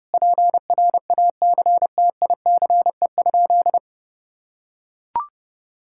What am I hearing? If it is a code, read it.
Morse code, heard as a keyed tone, about 30 wpm, 700 Hz. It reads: PRACTICE?
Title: Sentences from Top 1000 Words - Review Part 5 - 30wpm